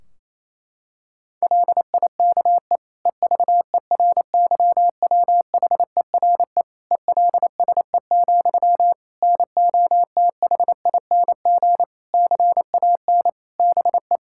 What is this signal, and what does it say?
Morse code, heard as a keyed tone, about 28 wpm, 700 Hz. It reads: LIKE EVERYWHERE ELSE, NOTHING CAN BE
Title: Sentences from Top 1600 Words 28wpm